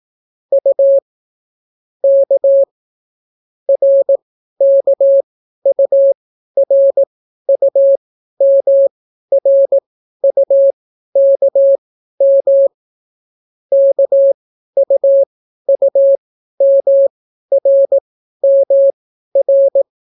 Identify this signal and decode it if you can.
Morse code, heard as a keyed tone, about 18 wpm, spaced as for 12 wpm, 550 Hz. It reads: U K RKURUMRUKM KUUMRMR